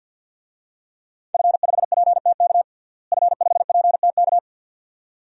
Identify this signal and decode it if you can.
Morse code, heard as a keyed tone, about 52 wpm, 700 Hz. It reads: F5PTK F5PTK
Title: DXCC-100 Call Signs - 5 Characters - Repeated 2x - 50wpm